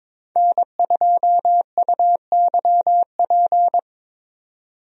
Morse code, heard as a keyed tone, about 23 wpm, 700 Hz. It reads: N2UYP
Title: USA Call Signs - 5 Characters - 22wpm